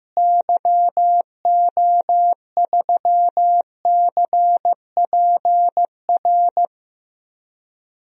Morse code, heard as a keyed tone, about 15 wpm, 700 Hz. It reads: YO3CPR
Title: DXCC-100 Call Signs - 6 Characters - Repeated 2x - 15wpm